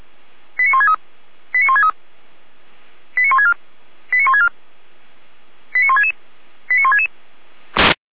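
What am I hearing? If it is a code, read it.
Five-tone selective call (ZVEI), heard as quick runs of five tones, tones of about 70 ms. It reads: 89163 89163 89164 89164 89166 89166
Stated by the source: ZVEI five-tone selective call received by SDR.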